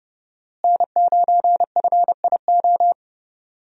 Morse code, heard as a keyed tone, about 29 wpm, 700 Hz. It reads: N9FIO
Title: USA Call Signs - 5 Characters - Repeated 3x - 30wpm